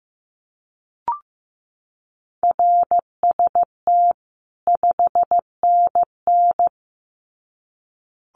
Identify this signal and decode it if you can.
Morse code, heard as a keyed tone, about 15 wpm, 700 Hz. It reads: RST 5NN
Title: Common QSO Phrases 15wpm